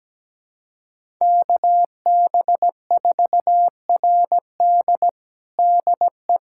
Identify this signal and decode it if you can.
Morse code, heard as a keyed tone, about 17 wpm, 700 Hz. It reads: KB4RD DE
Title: USA Callsign DE Callsign 17wpm